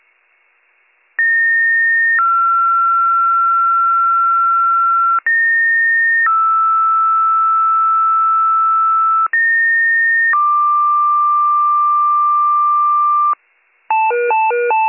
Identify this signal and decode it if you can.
Two-tone page, heard as a one-second tone, then a three-second tone, about 1850 then 1360 Hz, about 1850 then 1320 Hz, about 1850 then 1180 Hz. It is heard from 1.2 to 13.3 s.